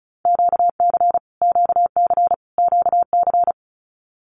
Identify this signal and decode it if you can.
Morse code, heard as a keyed tone, about 34 wpm, 700 Hz. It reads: QC QC QC
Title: Canadian Provinces and Territories - Repeated 3x - 35wpm